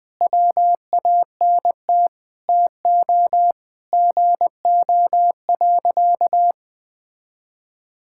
Morse code, heard as a keyed tone, about 19 wpm, 700 Hz. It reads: WANT TO GO.
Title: Sentences with 4-Word Phrases - 20wpm